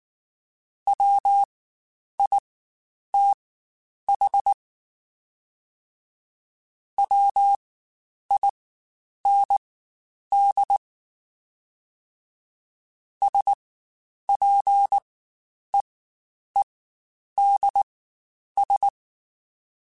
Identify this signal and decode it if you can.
Morse code, heard as a keyed tone, about 19 wpm, spaced as for 9 wpm, 800 Hz. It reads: WITH WIND SPEEDS
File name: Spaced Characters / Spaced Characters-mp3s / Weather 4 24 sp @18